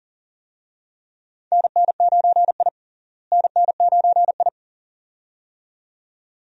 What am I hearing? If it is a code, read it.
Morse code, heard as a keyed tone, about 37 wpm, 700 Hz. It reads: NN9I NN9I
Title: USA Call Signs - 4 Characters - Repeated 2x - 40wpm